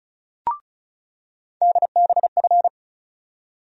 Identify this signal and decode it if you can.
Morse code, heard as a keyed tone, about 36 wpm, 700 Hz. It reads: DBF
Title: USA Call Sign Postfixes - 3 Letters 35wpm